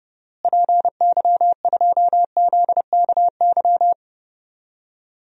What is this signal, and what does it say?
Morse code, heard as a keyed tone, about 30 wpm, 700 Hz. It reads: PY2ZKY
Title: DXCC-100 Call Signs - 6 Characters - Repeated 3x - 30wpm